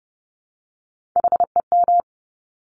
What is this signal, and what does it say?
Morse code, heard as a keyed tone, about 30 wpm, 700 Hz. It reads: HEM
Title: Top Three Letter Combinations in Words 30wpm